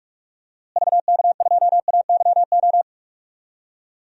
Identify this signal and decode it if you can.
Morse code, heard as a keyed tone, about 46 wpm, 700 Hz. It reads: VK2AYO
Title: DXCC-100 Call Signs - 6 Characters - Repeated 3x - 45wpm